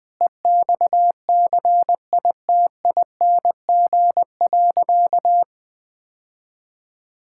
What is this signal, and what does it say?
Morse code, heard as a keyed tone, about 20 wpm, 700 Hz. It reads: EXCITING.
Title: Sentences with 5-Word Phrases - 20wpm